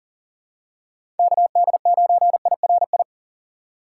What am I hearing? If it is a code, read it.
Morse code, heard as a keyed tone, about 40 wpm, 700 Hz. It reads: KD9IRI
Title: USA Call Signs - 6 Characters - 3x Character Spacing - 40wpm